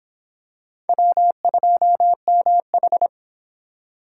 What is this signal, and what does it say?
Morse code, heard as a keyed tone, about 23 wpm, 700 Hz. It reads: W2MH